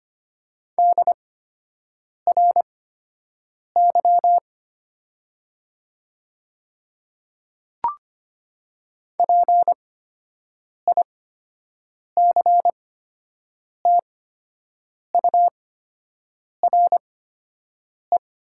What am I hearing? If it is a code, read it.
Morse code, heard as a keyed tone, about 25 wpm, spaced as for 7 wpm, 690 Hz. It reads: DRY PICTURE
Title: 8x Character Spacing - Top 500 Words - 25wpm